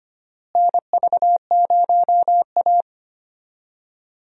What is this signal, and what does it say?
Morse code, heard as a keyed tone, about 25 wpm, 700 Hz. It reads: NV0A
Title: USA Call Signs - 4 Characters - Repeated 2x - 25wpm